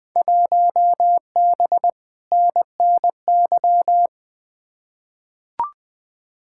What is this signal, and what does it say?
Morse code, heard as a keyed tone, about 21 wpm, 700 Hz. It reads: 1B NNY
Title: Field Day Exchanges - 20wpm